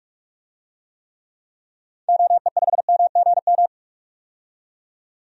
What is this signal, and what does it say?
Morse code, heard as a keyed tone, about 45 wpm, 700 Hz. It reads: OE5MGM
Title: DXCC-100 Call Signs - 6 Characters - 45wpm